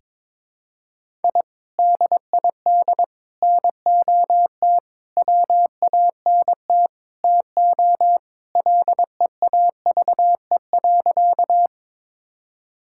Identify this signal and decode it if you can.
Morse code, heard as a keyed tone, about 22 wpm, 700 Hz. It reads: I DID NOT WANT TO LEAVE.
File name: practice / 22wpm / Sentences with 4-Word Phrases - 22wpm